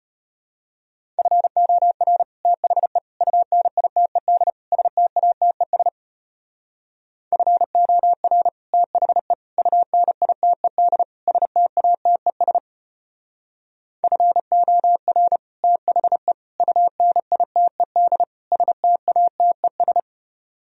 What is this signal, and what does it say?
Morse code, heard as a keyed tone, about 32 wpm, 690 Hz. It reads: FOR THE UNITED STATES FOR THE UNITED STATES FOR THE UNITED STATES